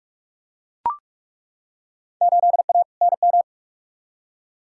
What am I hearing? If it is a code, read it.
Morse code, heard as a keyed tone, about 41 wpm, 700 Hz. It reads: 8A NM
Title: Field Day Exchanges Encore - 45wpm